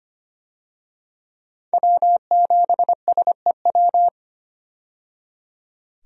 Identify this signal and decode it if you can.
Morse code, heard as a keyed tone, about 25 wpm, 700 Hz. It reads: W7SEW
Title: USA Call Signs - 5 Characters - 25wpm